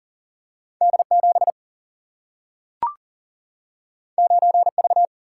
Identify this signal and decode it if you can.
Morse code, heard as a keyed tone, about 40 wpm, 700 Hz. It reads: D7 9V
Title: Single-Letter Single-Number 40wpm